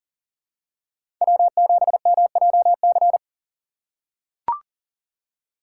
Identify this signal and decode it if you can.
Morse code, heard as a keyed tone, about 40 wpm, 700 Hz. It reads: W7MJC
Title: USA Call Signs - 5 Characters - 4x Character Spacing - 40wpm